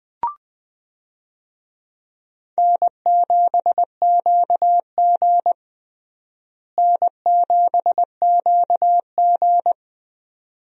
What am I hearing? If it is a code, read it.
Morse code, heard as a keyed tone, about 21 wpm, 700 Hz. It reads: N7QG N7QG
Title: USA Call Signs - 4 Characters - Repeated 2x - 20wpm